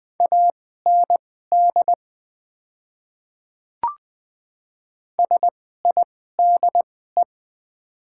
Morse code, heard as a keyed tone, about 20 wpm, spaced as for 14 wpm, 700 Hz. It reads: AND SIDE
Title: 2x Character Spacing - Top 100 Words - 20wpm